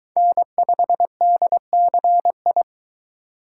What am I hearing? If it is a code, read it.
Morse code, heard as a keyed tone, about 22 wpm, 700 Hz. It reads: N5DCI